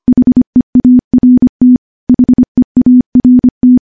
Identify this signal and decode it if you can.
Morse code, heard as a keyed tone, about 25 wpm, 250 Hz. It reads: HEART HEART
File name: morse-training-25WPM